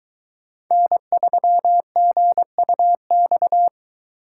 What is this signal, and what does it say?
Morse code, heard as a keyed tone, about 22 wpm, 700 Hz. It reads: N3GUX